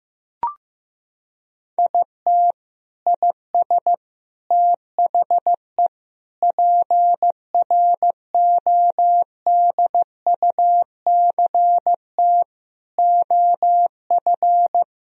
Morse code, heard as a keyed tone, about 15 wpm, 700 Hz. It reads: IT IS THE PRODUCT OF
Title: Sentences from Top 1000 Words - Review Part 2 - 15wpm